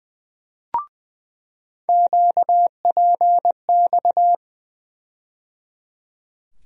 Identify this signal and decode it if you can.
Morse code, heard as a keyed tone, about 20 wpm, 700 Hz. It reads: QPX
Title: USA Call Sign Postfixes - 3 Letters 20wpm